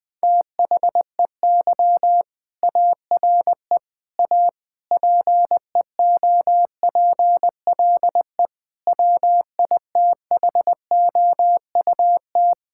Morse code, heard as a keyed tone, about 21 wpm, 700 Hz. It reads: THEY ARE A PEOPLE WITHOUT
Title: Sentences from Top 1000 Words - Review Part 6 - 20wpm